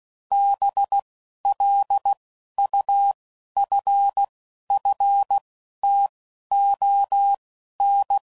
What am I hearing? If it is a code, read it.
Morse code, heard as a keyed tone, about 16 wpm, spaced as for 12 wpm, 800 Hz. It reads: BLUFFTON
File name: CW Code Groups @10WPM / fromSM's / Cities